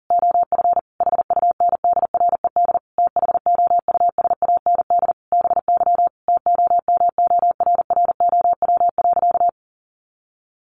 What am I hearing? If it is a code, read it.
Morse code, heard as a keyed tone, about 40 wpm, 700 Hz. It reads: OF HUNDRED THOUSAND BY TOMORROW.
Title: Sentences from Top 1100 Words 40wpm